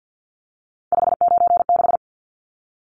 Morse code, heard as a keyed tone, about 50 wpm, 700 Hz. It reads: 596
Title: Numbers - 3-Digit 50wpm